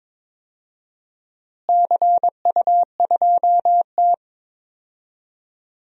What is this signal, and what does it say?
Morse code, heard as a keyed tone, about 22 wpm, 700 Hz. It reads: CU2T
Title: DXCC-100 Call Signs - 4 Characters - Repeated 3x - 22wpm